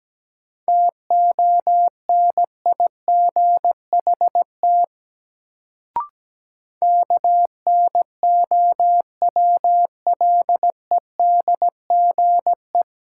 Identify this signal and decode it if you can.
Morse code, heard as a keyed tone, about 17 wpm, 700 Hz. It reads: TONIGHT KNOWLEDGE
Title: Sentences from Top 600 Words - New Word Drill - 17wpm